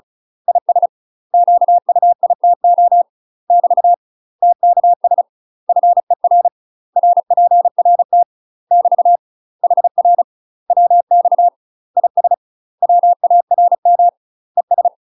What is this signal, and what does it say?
Morse code, heard as a keyed tone, about 36 wpm, 700 Hz. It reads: IS QUITO <BT> TKS FER RPRT <BT> HR WX IS WARM ES